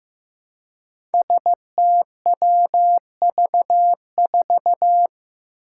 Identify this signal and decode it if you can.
Morse code, heard as a keyed tone, about 15 wpm, 700 Hz. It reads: STWV4
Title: Five Challenging Letter-Number Combinations 15wpm